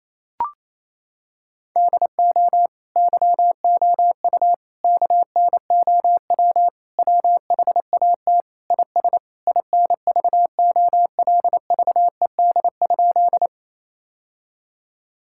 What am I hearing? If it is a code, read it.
Morse code, heard as a keyed tone, about 28 wpm, 700 Hz. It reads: DO YOU KNOW WHAT IS INVOLVED?
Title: Sentences from Top 1000 Words - Review Part 5 - 28wpm